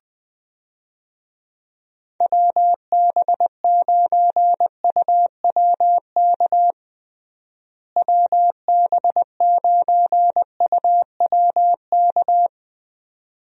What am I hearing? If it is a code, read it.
Morse code, heard as a keyed tone, about 20 wpm, 700 Hz. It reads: WB9UWK WB9UWK
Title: USA Call Signs - 6 Characters - Repeated 2x - 20wpm